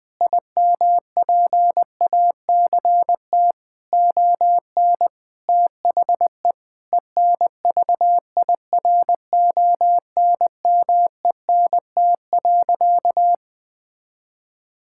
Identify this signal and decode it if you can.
Morse code, heard as a keyed tone, about 20 wpm, 700 Hz. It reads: IMPACT ON THE ENVIRONMENT.
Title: Sentences from Top 1000 Words - Review Part 6 - 20wpm